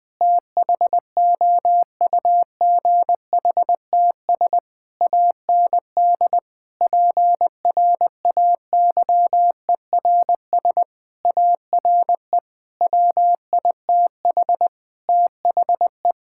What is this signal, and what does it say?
Morse code, heard as a keyed tone, about 20 wpm, 700 Hz. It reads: THOUGHTS AND PRAYERS ARE WITH THE
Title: Sentences with 7-Word Phrases - Spoken Beginning - 20wpm